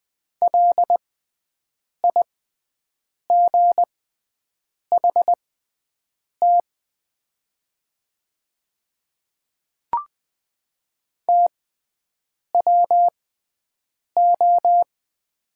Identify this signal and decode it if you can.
Morse code, heard as a keyed tone, about 20 wpm, spaced as for 7 wpm, 700 Hz. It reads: LIGHT TWO
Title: 6x Character Spacing - Top 200 Words - 20wpm